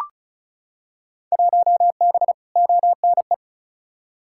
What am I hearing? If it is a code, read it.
Morse code, heard as a keyed tone, about 34 wpm, 700 Hz. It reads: 1B ONE
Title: Field Day Exchanges - 35wpm